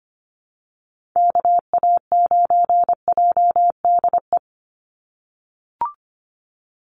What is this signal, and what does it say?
Morse code, heard as a keyed tone, about 25 wpm, 700 Hz. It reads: KA9JDE
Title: USA Call Signs 25wpm